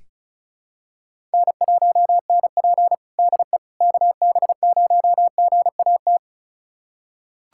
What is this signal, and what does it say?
Morse code, heard as a keyed tone, about 36 wpm, 700 Hz. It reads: N1NP DE KB0GAT
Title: USA Callsign DE Callsign 35wpm